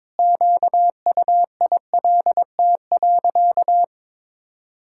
Morse code, heard as a keyed tone, about 22 wpm, 700 Hz. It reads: QUILT.
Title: Pangram Challenge v2 - 22wpm